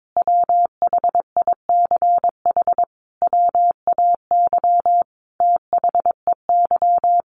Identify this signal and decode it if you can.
Morse code, heard as a keyed tone, about 22 wpm, 700 Hz. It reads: WHICH WAY THEY
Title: Sentences from Top 1300 Words 22wpm